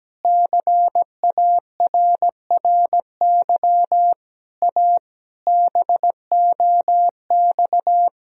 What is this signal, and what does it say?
Morse code, heard as a keyed tone, about 17 wpm, 700 Hz. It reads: CARRY A BOX
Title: Sentences from Top 1000 Words - Review Part 2 - 17wpm